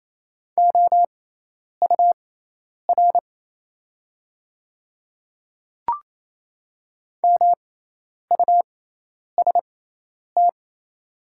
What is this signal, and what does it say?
Morse code, heard as a keyed tone, about 28 wpm, spaced as for 10 wpm, 700 Hz. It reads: OUR MUST
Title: 6x Character Spacing - Top 300 Words - 28wpm